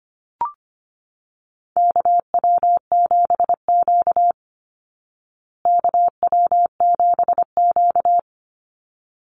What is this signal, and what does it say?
Morse code, heard as a keyed tone, about 25 wpm, 700 Hz. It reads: KW7Q KW7Q
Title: Call Signs Repeated 3x - 25wpm